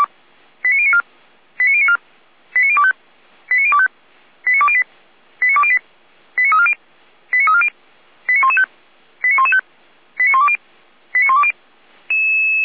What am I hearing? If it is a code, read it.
Five-tone selective call (ZVEI), heard as quick runs of five tones, tones of about 70 ms. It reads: 89094 89094 89025 89025 89208 89208 89340 89340 89115 89115 89120 89120
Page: SDR/ZVEI